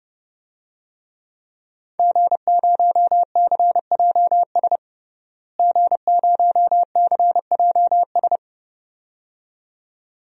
Morse code, heard as a keyed tone, about 30 wpm, 700 Hz. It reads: G0CJS G0CJS
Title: DXCC-100 Call Signs - 5 Characters - Repeated 2x - 30wpm